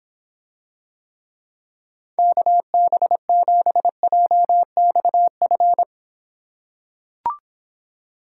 Morse code, heard as a keyed tone, about 26 wpm, 700 Hz. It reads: KB7JXF